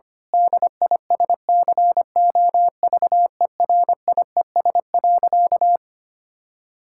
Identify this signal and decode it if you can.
Morse code, heard as a keyed tone, about 25 wpm, 700 Hz. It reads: DISCOVERIES.
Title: Pangram Challenge v2 - 25wpm